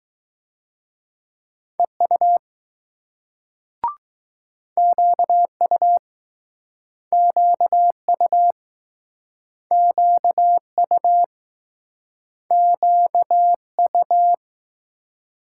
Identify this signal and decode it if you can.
Morse code, heard as a keyed tone, about 22 wpm, 700 Hz. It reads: EU QU QU QU QU